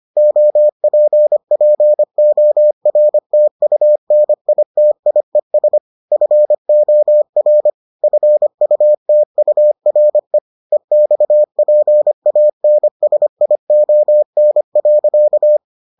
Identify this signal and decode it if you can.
Morse code, heard as a keyed tone, about 25 wpm, 600 Hz. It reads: OPPORTUNITIES FOR FUTURE EXPANSION.